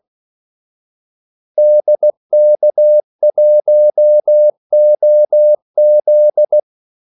Morse code, heard as a keyed tone, about 16 wpm, 600 Hz. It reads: DK1OZ
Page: Kaldesignaler 16 wpm